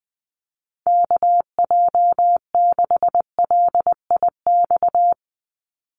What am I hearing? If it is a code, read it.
Morse code, heard as a keyed tone, about 20 wpm, 700 Hz. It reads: KJ6LIX
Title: USA Call Signs 20wpm